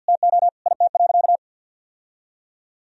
Morse code, heard as a keyed tone, about 50 wpm, 700 Hz. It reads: TO IT.
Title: Sentences from Top 1000 Words - Review Part 3 - 50wpm